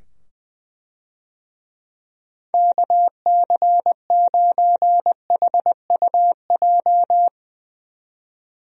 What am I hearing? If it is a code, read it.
Morse code, heard as a keyed tone, about 20 wpm, 700 Hz. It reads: KC9HUJ
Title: USA Call Signs - 6 Characters - 20wpm